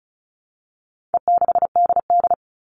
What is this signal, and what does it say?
Morse code, heard as a keyed tone, about 34 wpm, 700 Hz. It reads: E6DD
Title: All Cut Numbers - 4-Digit 35wpm